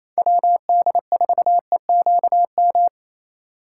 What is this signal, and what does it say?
Morse code, heard as a keyed tone, about 33 wpm, 700 Hz. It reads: WD4EQM
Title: Speed Racing - USA 2x3 Call Signs 28wpm